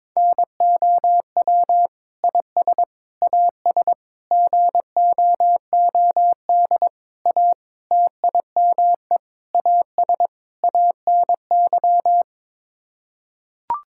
Morse code, heard as a keyed tone, about 22 wpm, 700 Hz. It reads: NOW IS AS GOOD A TIME AS ANY
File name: practice / 22wpm / Sets of 8 Words - 22wpm